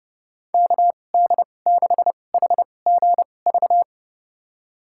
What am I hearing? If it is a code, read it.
Morse code, heard as a keyed tone, about 29 wpm, spaced as for 21 wpm, 700 Hz. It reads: KD6HGV
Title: USA Call Signs - 6 Characters - 2x Character Spacing - 30wpm